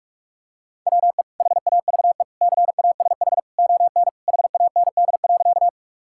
Morse code, heard as a keyed tone, about 46 wpm, 700 Hz. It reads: WE HAVE CASH ON HAND.
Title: Sentences from Top 1000 Words - Review Part 5 - 45wpm